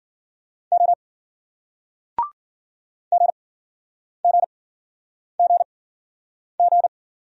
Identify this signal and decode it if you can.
Morse code, heard as a keyed tone, about 53 wpm, spaced as for 9 wpm, 700 Hz. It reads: O GGGG